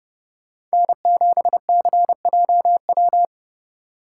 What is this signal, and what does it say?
Morse code, heard as a keyed tone, about 30 wpm, 700 Hz. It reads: N7CJW
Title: Speed Racing - USA 1x3 Call Signs 25wpm